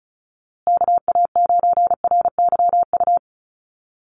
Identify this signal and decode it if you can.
Morse code, heard as a keyed tone, about 34 wpm, 700 Hz. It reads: KA9RYU
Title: USA Call Signs Repeated 2x 35wpm